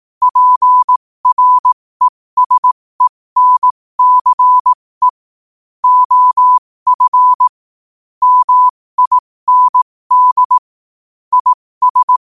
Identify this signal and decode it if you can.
Morse code, heard as a keyed tone, about 18 wpm, spaced as for 15 wpm, 1000 Hz. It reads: PRESENCE OF MIND IS
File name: qotdfiles / 15wpm